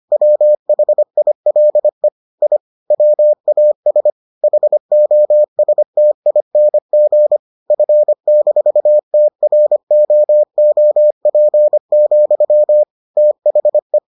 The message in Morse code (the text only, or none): WHILE I WAS HOSTING F-TROOP, THE